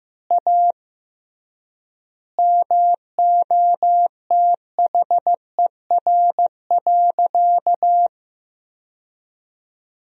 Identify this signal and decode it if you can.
Morse code, heard as a keyed tone, about 16 wpm, 700 Hz. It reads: A MOTHER.
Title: 3x Word-Spacing - Sentences from Top 200 Words - 15wpm